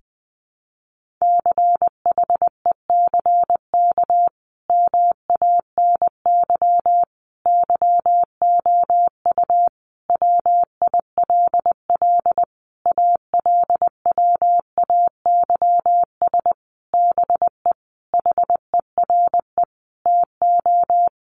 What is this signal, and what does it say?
Morse code, heard as a keyed tone, about 20 wpm, 700 Hz. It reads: CHECK MANY YOU WILL ALWAYS BE HERE TO
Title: Sentences from Top 1500 Words 20wpm